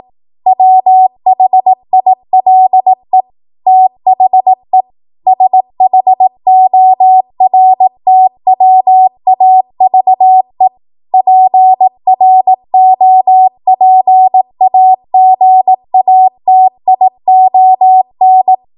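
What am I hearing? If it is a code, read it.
Morse code, heard as a keyed tone, about 18 wpm, 750 Hz. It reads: WHILE THE SHORTWAVE PROPAGATION
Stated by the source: Code practice at 18 WPM, a transition run bridging 15 to 20 WPM.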